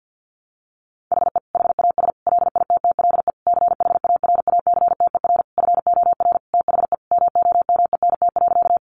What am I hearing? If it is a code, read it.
Morse code, heard as a keyed tone, about 50 wpm, 700 Hz. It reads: HE HAS LITTLE CHARACTER FOR THE MOMENT.
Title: Sentences from Top 1200 Words 50wpm